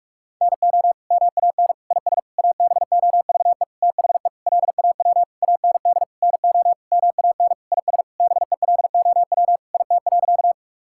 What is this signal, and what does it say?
Morse code, heard as a keyed tone, about 46 wpm, 700 Hz. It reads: NO MAN IS ABOVE THE LAW AND NO MAN IS BELOW IT.
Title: Sentences from Top 1000 Words - Review Part 5 - 45wpm